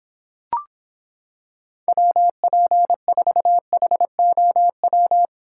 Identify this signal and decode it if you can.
Morse code, heard as a keyed tone, about 26 wpm, 700 Hz. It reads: WP4HOW